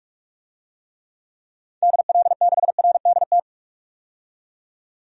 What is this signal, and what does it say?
Morse code, heard as a keyed tone, about 47 wpm, 700 Hz. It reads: DL6RDT
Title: DXCC-100 Call Signs - 6 Characters - 45wpm